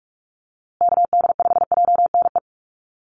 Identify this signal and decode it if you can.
Morse code, heard as a keyed tone, about 44 wpm, 690 Hz. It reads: KD5JNE
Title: USA Call Signs Repeated 3x 45wpm